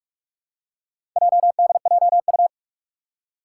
Morse code, heard as a keyed tone, about 44 wpm, 700 Hz. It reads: JDJU
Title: Four Challenging Letter-Number Combinations 45wpm